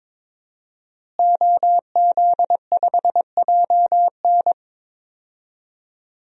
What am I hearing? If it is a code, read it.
Morse code, heard as a keyed tone, about 22 wpm, 700 Hz. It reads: OZ5JN
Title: DXCC-100 Call Signs - 5 Characters - Repeated 3x - 22wpm